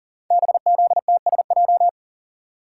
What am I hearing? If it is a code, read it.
Morse code, heard as a keyed tone, about 42 wpm, 690 Hz. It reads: BZTSJ
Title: Code Groups - 5 Letters - 40wpm